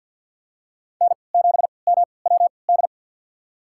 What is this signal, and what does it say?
Morse code, heard as a keyed tone, about 53 wpm, spaced as for 29 wpm, 700 Hz. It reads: N7MWD
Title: USA Call Signs - 5 Characters - 3x Character Spacing - 50wpm